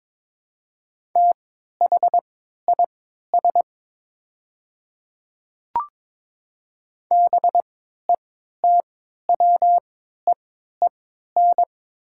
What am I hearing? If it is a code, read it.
Morse code, heard as a keyed tone, about 22 wpm, spaced as for 13 wpm, 700 Hz. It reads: THIS BETWEEN
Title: 3x Character Spacing - Top 300 Words - 22wpm